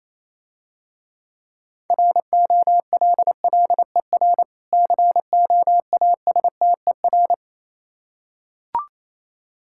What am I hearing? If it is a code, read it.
Morse code, heard as a keyed tone, about 27 wpm, 700 Hz. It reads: ROLLER COASTER
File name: Easy Words - Speed-Racing with Hints - 22wpm